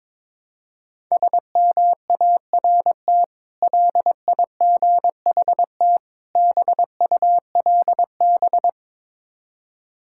Morse code, heard as a keyed tone, about 22 wpm, 690 Hz. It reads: SMART LIGHT BULB
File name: Difficult Words with Hints - 22wpm